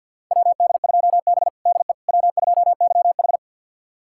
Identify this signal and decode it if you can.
Morse code, heard as a keyed tone, about 51 wpm, 700 Hz. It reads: WD2B DE W2YH
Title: USA Callsign DE Callsign - 4 Characters - 50wpm